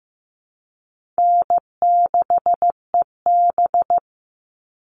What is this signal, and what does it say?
Morse code, heard as a keyed tone, about 15 wpm, 700 Hz. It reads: N6EB
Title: All Cut Numbers - 4-Digit 15wpm